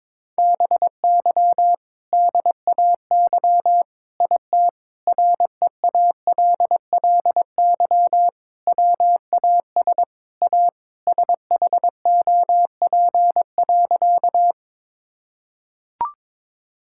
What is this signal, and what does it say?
Morse code, heard as a keyed tone, about 22 wpm, 700 Hz. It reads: BY DAY IT REALLY WAS A SHOP.
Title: Sentences from Top 1600 Words 22wpm